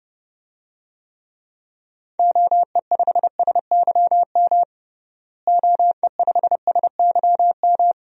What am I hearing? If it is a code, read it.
Morse code, heard as a keyed tone, about 30 wpm, 700 Hz. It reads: OE5SYM OE5SYM
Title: DXCC-100 Call Signs - 6 Characters - Repeated 2x - 30wpm